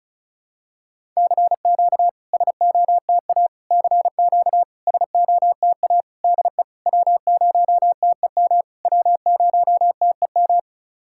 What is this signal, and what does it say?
Morse code, heard as a keyed tone, about 34 wpm, 690 Hz. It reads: CQ SOTA CQ SOTA DE W0TEM W0TEM
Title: SOTA QSO Practice 35wpm